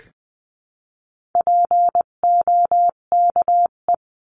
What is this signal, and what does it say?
Morse code, heard as a keyed tone, about 19 wpm, 700 Hz. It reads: POKE
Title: 258 Lesson 42c - Words 16-20wpm